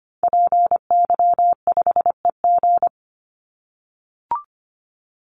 Morse code, heard as a keyed tone, about 25 wpm, 700 Hz. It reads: PY5EG
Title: Call Signs - 25wpm